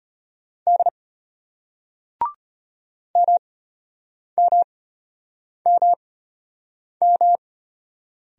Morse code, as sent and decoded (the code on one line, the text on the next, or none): -.. / -- -- -- --
D MMMM